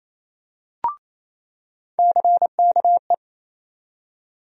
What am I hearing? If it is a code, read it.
Morse code, heard as a keyed tone, about 27 wpm, 700 Hz. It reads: CKE
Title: USA Call Sign Postfixes - 3 Letters 28wpm